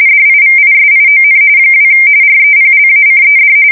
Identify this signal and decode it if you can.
RTTY, about 45.45 baud. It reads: DIVISION 2 REPLACED
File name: fsk170 / USS-Missouri